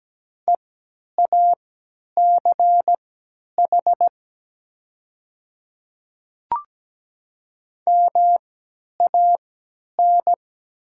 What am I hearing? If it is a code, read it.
Morse code, heard as a keyed tone, about 17 wpm, spaced as for 10 wpm, 700 Hz. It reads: EACH MAN
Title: 3x Character Spacing - Top 200 Words - 17wpm